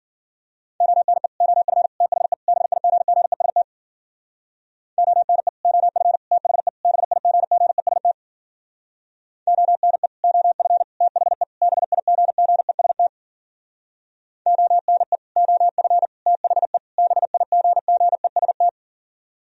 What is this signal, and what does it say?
Morse code, heard as a keyed tone, about 52 wpm, 700 Hz. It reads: ONE OF THE BIGGEST ONE OF THE BIGGEST ONE OF THE BIGGEST ONE OF THE BIGGEST